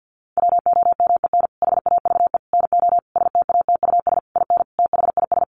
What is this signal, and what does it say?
Morse code, heard as a keyed tone, about 50 wpm, 700 Hz. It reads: WOMEN HAVE NO STATUS IN THIS
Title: Sentences from Top 1400 Words 50wpm